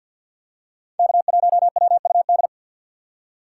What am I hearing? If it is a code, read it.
Morse code, heard as a keyed tone, about 52 wpm, 700 Hz. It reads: K1WUD
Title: USA Call Signs - 5 Characters - 2x Character Spacing - 50wpm